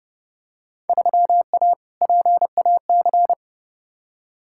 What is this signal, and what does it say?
Morse code, heard as a keyed tone, about 30 wpm, 700 Hz. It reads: 3A PAC
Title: Field Day Exchanges - 30wpm